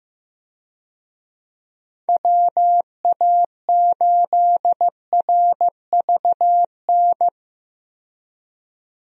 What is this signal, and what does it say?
Morse code, heard as a keyed tone, about 15 wpm, 700 Hz. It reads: WA8RVN
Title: USA Call Signs - 6 Characters - 15wpm